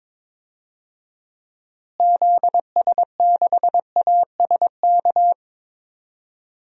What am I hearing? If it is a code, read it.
Morse code, heard as a keyed tone, about 22 wpm, 700 Hz. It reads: ZS6ASK
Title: DXCC-100 Call Signs - 6 Characters - Repeated 3x - 22wpm